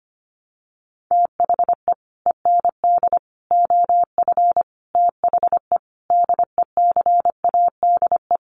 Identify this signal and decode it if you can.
Morse code, heard as a keyed tone, about 25 wpm, 700 Hz. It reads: THE END OF THE DECADE
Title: Sets of 5 Words Encore 25wpm